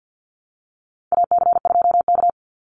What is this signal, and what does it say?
Morse code, heard as a keyed tone, about 50 wpm, 700 Hz. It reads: AC2K
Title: Call Signs Repeated 3x - 50wpm